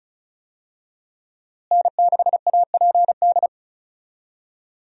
Morse code, heard as a keyed tone, about 35 wpm, 700 Hz. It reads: N6APD
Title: USA Call Signs - 5 Characters - Repeated 2x - 35wpm